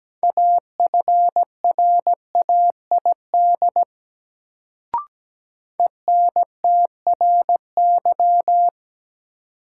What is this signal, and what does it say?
Morse code, heard as a keyed tone, about 17 wpm, 700 Hz. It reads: AFRAID ENTRY
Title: Sentences from Top 1500 Words - New Word Drill Alt - 17wpm